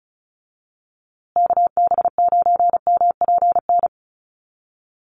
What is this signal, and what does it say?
Morse code, heard as a keyed tone, about 34 wpm, 700 Hz. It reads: KB9MPN
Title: USA Call Signs Repeated 3x 35wpm